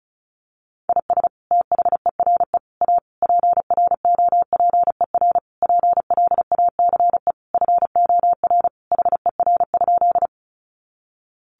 Morse code, heard as a keyed tone, about 34 wpm, 700 Hz. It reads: IS THERE A PROPER PLACE FOR HER?
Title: Sentences from Top 800 Words 35wpm